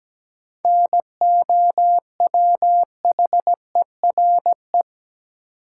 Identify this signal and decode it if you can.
Morse code, heard as a keyed tone, about 17 wpm, 700 Hz. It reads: NOWHERE
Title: Sets of 5 Words Encore - New Word Drill - 17wpm